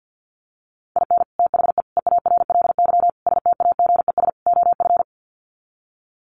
Morse code, heard as a keyed tone, about 50 wpm, 700 Hz. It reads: IN THE EARLY STAGES OF
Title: Sets of 5 Words Encore 50wpm